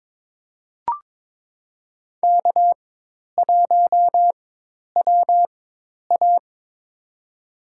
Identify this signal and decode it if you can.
Morse code, heard as a keyed tone, about 22 wpm, spaced as for 10 wpm, 700 Hz. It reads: K1WA
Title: USA Call Signs - 4 Characters - 4x Character Spacing - 22wpm